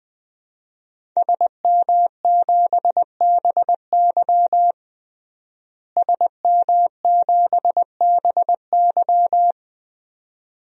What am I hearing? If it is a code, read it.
Morse code, heard as a keyed tone, about 21 wpm, 700 Hz. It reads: SM7BY SM7BY
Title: DXCC-100 Call Signs - 5 Characters - Repeated 2x - 20wpm